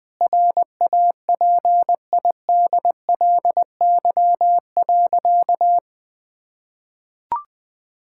Morse code, heard as a keyed tone, about 20 wpm, 700 Hz. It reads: RAPIDLY.
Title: Sentences with 5-Word Phrases - 20wpm